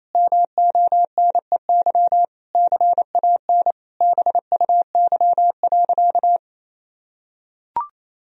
Morse code, heard as a keyed tone, about 28 wpm, 700 Hz. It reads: MONEY CAN BUY.
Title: Sentences from Top 1000 Words - Review Part 4 - 28wpm